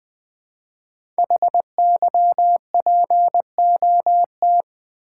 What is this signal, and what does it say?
Morse code, heard as a keyed tone, about 20 wpm, 700 Hz. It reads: HYPOT
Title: Build A Word - 5 Letters At A Time - 10-Letter Words - 20wpm